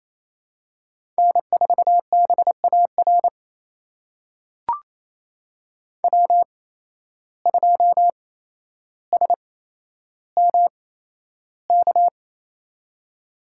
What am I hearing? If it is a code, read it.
Morse code, heard as a keyed tone, about 28 wpm, spaced as for 12 wpm, 700 Hz. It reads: N4BAR W2SMK
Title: USA Call Signs - 5 Characters - 8x Character Spacing - 28wpm